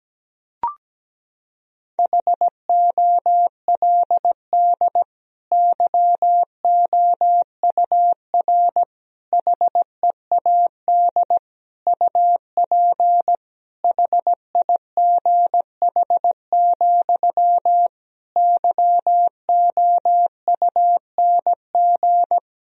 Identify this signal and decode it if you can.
Morse code, heard as a keyed tone, about 17 wpm, 700 Hz. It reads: HOLD YOUR HEAD UP HIGH, YOUNG
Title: Sentences from Top 1000 Words - Review Part 2 - 17wpm